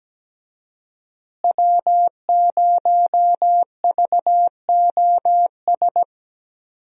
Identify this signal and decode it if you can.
Morse code, heard as a keyed tone, about 17 wpm, 700 Hz. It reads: W0VOS
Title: USA Call Signs - 5 Characters - Repeated 3x - 17wpm